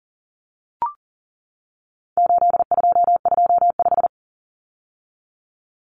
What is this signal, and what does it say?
Morse code, heard as a keyed tone, about 40 wpm, 700 Hz. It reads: 8225
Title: Numbers - 4-Digit 40wpm